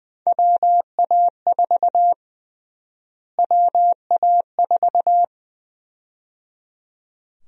Morse code, heard as a keyed tone, about 20 wpm, 700 Hz. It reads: WA4 WA4
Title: USA Call Sign Prefixes - Repeated 2x - 20wpm